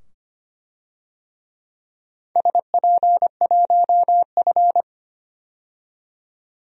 Morse code, heard as a keyed tone, about 25 wpm, 700 Hz. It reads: SP1F
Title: DXCC-100 Call Signs - 4 Characters - 25wpm